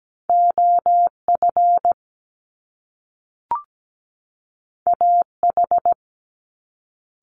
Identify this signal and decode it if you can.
Morse code, heard as a keyed tone, about 17 wpm, 700 Hz. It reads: OF AH
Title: Top Two Letter Combinations in Words 17wpm